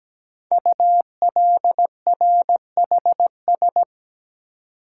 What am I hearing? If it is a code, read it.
Morse code, heard as a keyed tone, about 17 wpm, 700 Hz. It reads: ULRHS
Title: Code Groups - 5 Letters - 17wpm